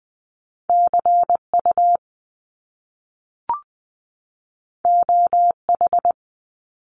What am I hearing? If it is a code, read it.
Morse code, heard as a keyed tone, about 20 wpm, 700 Hz. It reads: C U O H
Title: Top Two Letter Combinations in Words 20wpm